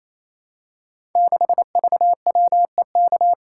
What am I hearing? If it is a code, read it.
Morse code, heard as a keyed tone, about 27 wpm, 700 Hz. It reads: 6VWEK
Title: Code Groups - 5 Letter:Numbers - 28wpm